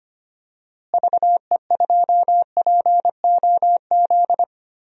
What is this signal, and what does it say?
Morse code, heard as a keyed tone, about 25 wpm, 700 Hz. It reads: VE2POZ
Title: DXCC-100 Call Signs - 6 Characters - Repeated 2x - 25wpm